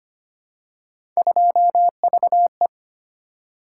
Morse code, heard as a keyed tone, about 25 wpm, 700 Hz. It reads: 2VE
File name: Three Letter-Number Combinations - 25wpm